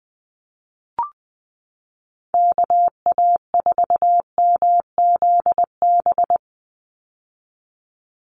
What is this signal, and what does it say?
Morse code, heard as a keyed tone, about 20 wpm, 700 Hz. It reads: KA4MZB
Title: USA Call Signs 20wpm